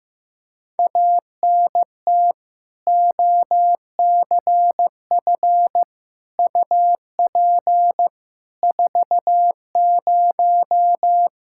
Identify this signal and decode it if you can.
Morse code, heard as a keyed tone, about 16 wpm, 700 Hz. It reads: ANT OCF UP 40
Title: K1USN SST QSO Practice 15wpm